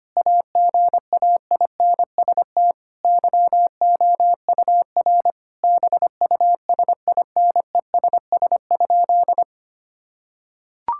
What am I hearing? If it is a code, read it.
Morse code, heard as a keyed tone, about 25 wpm, 700 Hz. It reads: AGAINST YOUR BUSINESS?
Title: Sentences from Top 1000 Words - Review Part 5 - 25wpm